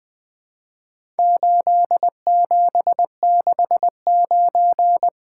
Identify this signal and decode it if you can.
Morse code, heard as a keyed tone, about 22 wpm, 700 Hz. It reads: 8769
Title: Speed-Racing - Four Letter-Number Combinations - 20wpm